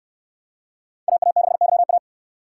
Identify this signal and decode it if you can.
Morse code, heard as a keyed tone, about 62 wpm, 700 Hz. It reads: AA6YA